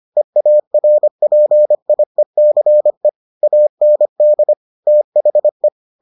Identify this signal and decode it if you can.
Morse code, heard as a keyed tone, about 25 wpm, 600 Hz. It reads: EARPIECE AND THE